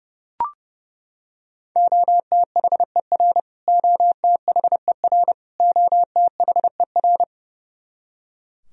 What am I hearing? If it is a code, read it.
Morse code, heard as a keyed tone, about 30 wpm, 700 Hz. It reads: OTHER OTHER OTHER
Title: Top 100 Words Repeated 30wpm